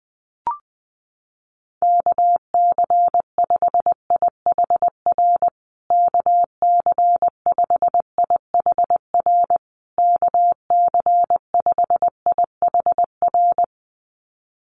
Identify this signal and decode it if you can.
Morse code, heard as a keyed tone, about 19 wpm, 700 Hz. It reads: KC5IHR KC5IHR KC5IHR
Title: USA Call Signs Repeated 3x 20wpm